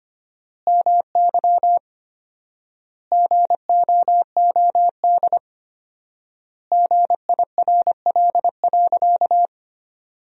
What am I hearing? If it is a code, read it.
Morse code, heard as a keyed tone, about 25 wpm, 700 Hz. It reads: MY GOOD GIRL.
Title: 4x Word-Spacing - Sentences from Top 400 Words - 25wpm